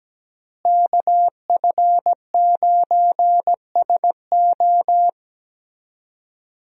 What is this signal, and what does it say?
Morse code, heard as a keyed tone, about 17 wpm, 700 Hz. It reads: KF9SO
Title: USA Call Signs - 5 Characters - Repeated 3x - 17wpm